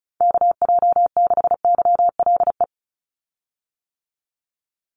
Morse code, heard as a keyed tone, about 34 wpm, 700 Hz. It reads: KJ6YLE
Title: USA Call Signs 35wpm